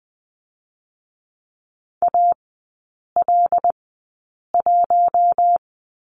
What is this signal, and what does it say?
Morse code, heard as a keyed tone, about 20 wpm, 700 Hz. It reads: A L 1
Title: 107 Lesson 18e - Letter m - Three Characters 20wpm